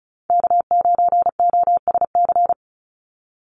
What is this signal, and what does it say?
Morse code, heard as a keyed tone, about 34 wpm, 700 Hz. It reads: K9OSC
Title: NA SOTA Chaser Callsigns 35wpm